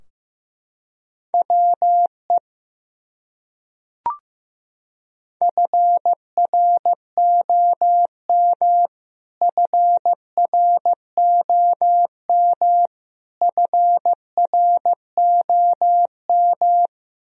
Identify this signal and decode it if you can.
Morse code, heard as a keyed tone, about 15 wpm, 700 Hz. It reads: WE FROM FROM FROM
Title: Top 100 Words Repeated 15wpm